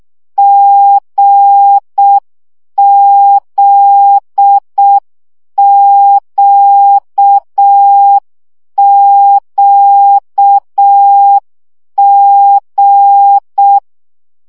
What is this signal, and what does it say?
Morse code, heard as a keyed tone, about 6 wpm, 800 Hz. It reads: GZQQG
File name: CW gyakorlas g z q 5 baud